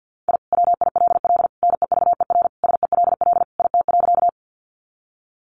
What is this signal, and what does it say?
Morse code, heard as a keyed tone, about 50 wpm, 700 Hz. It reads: I WILL NEVER SELL IT.
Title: Sentences from Top 1100 Words 50wpm